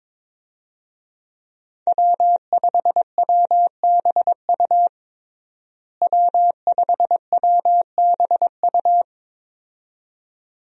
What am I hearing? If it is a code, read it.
Morse code, heard as a keyed tone, about 22 wpm, 700 Hz. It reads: W5WBU W5WBU
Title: USA Call Signs - 5 Characters - Repeated 2x - 22wpm